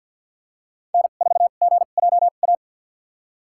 Morse code, heard as a keyed tone, about 53 wpm, spaced as for 37 wpm, 700 Hz. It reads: N4GJA
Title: USA Call Signs - 5 Characters - 2x Character Spacing - 50wpm